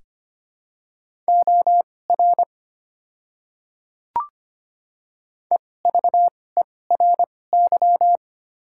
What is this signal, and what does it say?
Morse code, heard as a keyed tone, about 25 wpm, spaced as for 18 wpm, 700 Hz. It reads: OR EVERY
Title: 2x Character Spacing - Top 200 Words - 25wpm